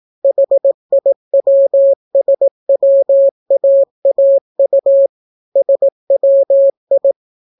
Morse code, heard as a keyed tone, about 18 wpm, 550 Hz. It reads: HIWSWAAU SWI